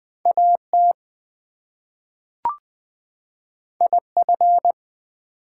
Morse code, heard as a keyed tone, about 20 wpm, 700 Hz. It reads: AT IF
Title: Two Letter Words - 20wpm